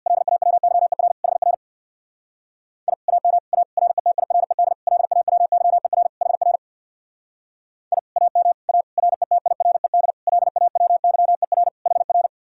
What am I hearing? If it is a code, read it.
Morse code, heard as a keyed tone, about 62 wpm, 700 Hz. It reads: LAWYER HR I AM A RETIRED LAWYER HR I AM A RETIRED LAWYER HR
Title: Speed-Racing Common QSO Phrases - 50wpm